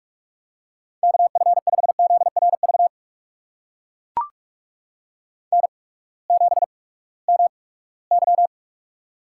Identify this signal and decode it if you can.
Morse code, heard as a keyed tone, about 44 wpm, spaced as for 22 wpm, 700 Hz. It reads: KF5ZRV N7MY
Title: USA Call Signs - 8x Character Spacing - 45wpm